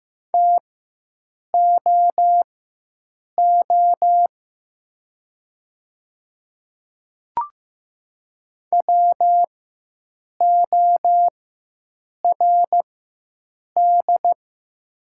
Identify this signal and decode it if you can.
Morse code, heard as a keyed tone, about 15 wpm, spaced as for 7 wpm, 700 Hz. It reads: TOO WORD
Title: 4x Character Spacing - Top 300 Words - 15wpm